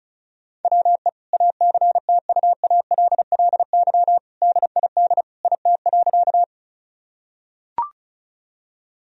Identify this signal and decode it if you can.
Morse code, heard as a keyed tone, about 34 wpm, 700 Hz. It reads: WE ACTUALLY DID IT.
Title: Sentences from Top 1000 Words - Review Part 4 - 35wpm